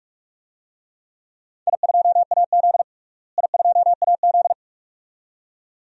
Morse code, heard as a keyed tone, about 46 wpm, 700 Hz. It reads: I2AZ I2AZ
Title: DXCC-100 Call Signs - 4 Characters - Repeated 2x - 45wpm